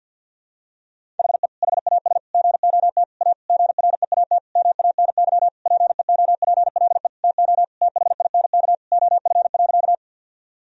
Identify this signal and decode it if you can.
Morse code, heard as a keyed tone, about 50 wpm, 700 Hz. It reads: HE HAS GOT A GREAT MANY PEOPLE TO THINK OF.
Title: Sentences from Top 1000 Words - Review Part 1 - 50wpm